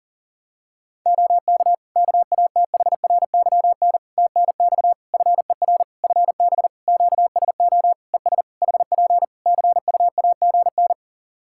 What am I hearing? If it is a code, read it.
Morse code, heard as a keyed tone, about 40 wpm, 700 Hz. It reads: OK KATHRYN TNX FER FB QSO ES HP CUAGN
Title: K1USN SST QSO Practice 40wpm